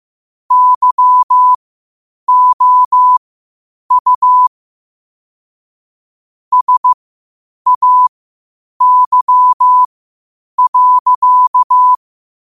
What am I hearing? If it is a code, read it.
Morse code, heard as a keyed tone, about 15 wpm, spaced as for 8 wpm, 1000 Hz. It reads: YOU SAY.
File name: qotdfiles / 8wpm